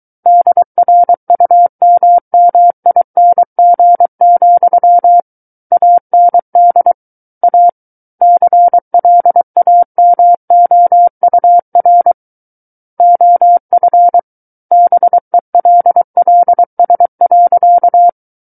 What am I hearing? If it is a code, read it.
Morse code, heard as a keyed tone, about 23 wpm, 700 Hz. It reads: DRUMMING, AND A CLAMOUR OF BELLS.
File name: Book One / 14-Chapter 14 - 23 WPM